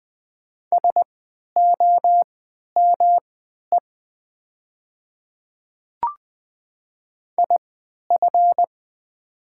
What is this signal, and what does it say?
Morse code, heard as a keyed tone, about 20 wpm, spaced as for 11 wpm, 700 Hz. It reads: SOME IF
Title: 3x Character Spacing - Top 100 Words - 20wpm